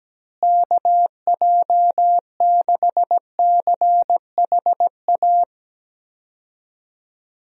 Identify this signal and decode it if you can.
Morse code, heard as a keyed tone, about 17 wpm, 700 Hz. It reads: KJ6CHA
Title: USA Call Signs - 6 Characters - Repeated 2x - 17wpm